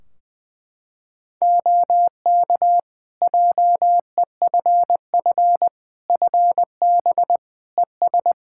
Morse code, heard as a keyed tone, about 21 wpm, 700 Hz. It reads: OK JEFF FB ES
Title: K1USN SST QSO Practice 20wpm